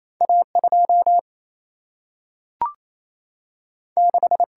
Morse code, heard as a keyed tone, about 28 wpm, 700 Hz. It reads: A 2 6
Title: Single-Letter Single-Number 28wpm